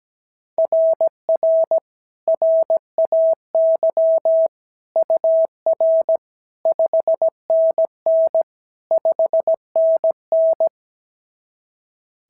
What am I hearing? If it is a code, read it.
Morse code, heard as a keyed tone, about 16 wpm, 650 Hz. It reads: RR RAY UR 5NN 5NN
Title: SOTA QSO Practice 17wpm